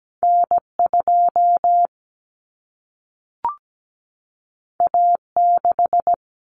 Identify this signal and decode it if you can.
Morse code, heard as a keyed tone, about 17 wpm, 700 Hz. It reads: N2 A6
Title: Common Cut Numbers - 2-Digit 17wpm